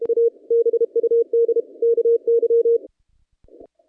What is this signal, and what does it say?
Morse code, heard as a keyed tone, about 32 wpm, 460 Hz. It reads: A BUD KY
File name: contests / naqp-cw-k5tr-op-k5pi- -0300